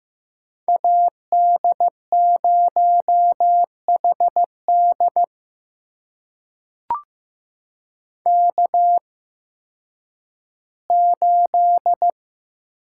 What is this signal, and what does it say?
Morse code, heard as a keyed tone, about 15 wpm, spaced as for 10 wpm, 700 Hz. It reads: AD0HD K8
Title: USA Call Signs - 5 Characters - 8x Character Spacing - 15wpm